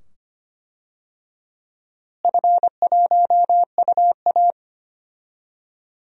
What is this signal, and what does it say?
Morse code, heard as a keyed tone, about 25 wpm, 700 Hz. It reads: F1UA
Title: DXCC-100 Call Signs - 4 Characters - Repeated 2x - 25wpm